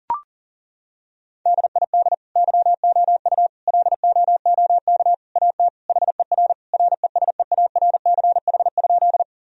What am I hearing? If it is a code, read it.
Morse code, heard as a keyed tone, about 37 wpm, 700 Hz. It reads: DID YOU LOOK AT HER RESEARCH?
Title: Sentences from Top 1000 Words - Review Part 4 - 40wpm